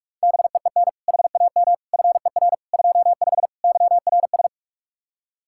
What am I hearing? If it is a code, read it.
Morse code, heard as a keyed tone, about 50 wpm, 700 Hz. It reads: BEEN HAM FER 25 YRS